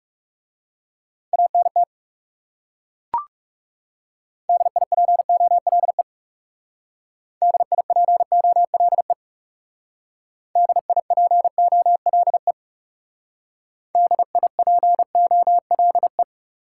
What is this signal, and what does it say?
Morse code, heard as a keyed tone, about 40 wpm, 700 Hz. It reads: ANT DIPOLE DIPOLE DIPOLE DIPOLE